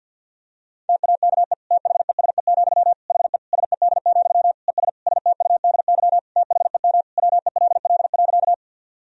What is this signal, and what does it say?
Morse code, heard as a keyed tone, about 52 wpm, 700 Hz. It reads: TAKE THESE, HE SED, ES STUDY THEM WELL.
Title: Sentences from Top 300 Words and Abbreviations 50wpm